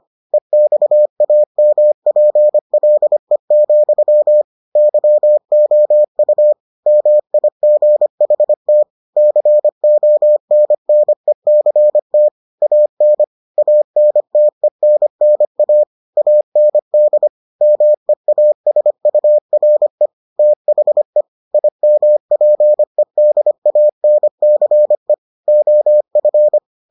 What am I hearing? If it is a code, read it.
Morse code, heard as a keyed tone, about 25 wpm, 600 Hz. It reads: EXAMPLE, YOU MIGHT CONNECT AN ANTENNA AND MEASURE THE IMPEDANCE OF